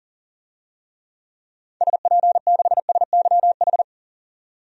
Morse code, heard as a keyed tone, about 40 wpm, 700 Hz. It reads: SP6SYH
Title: DXCC-100 Call Signs - 6 Characters - 40wpm